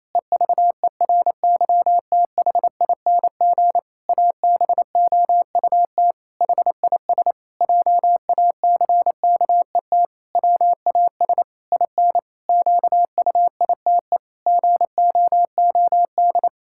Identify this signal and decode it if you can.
Morse code, heard as a keyed tone, about 28 wpm, 700 Hz. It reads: EVERYTHING ABOUT HIS JACKET WAS IN QUITE GOOD
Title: Pangram Challenge v2 - 28wpm